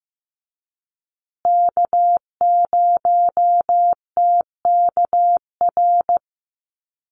Morse code, heard as a keyed tone, about 15 wpm, 700 Hz. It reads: K0TKR
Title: USA Call Signs Repeated 2x 15wpm